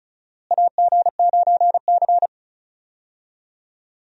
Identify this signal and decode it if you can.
Morse code, heard as a keyed tone, about 34 wpm, 700 Hz. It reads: AG9C
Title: USA Call Signs - 4 Characters - Repeated 2x - 35wpm